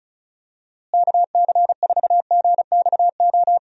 Morse code, heard as a keyed tone, about 36 wpm, 700 Hz. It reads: KC4GXO
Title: USA Call Signs - 6 Characters - Repeated 3x - 35wpm